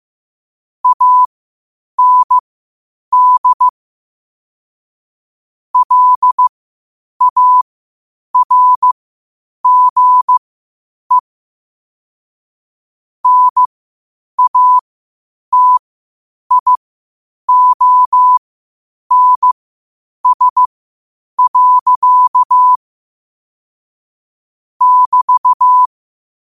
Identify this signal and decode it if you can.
Morse code, heard as a keyed tone, about 15 wpm, spaced as for 8 wpm, 1000 Hz. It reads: AND LARGE NATIONS. <BT>
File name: 8wpm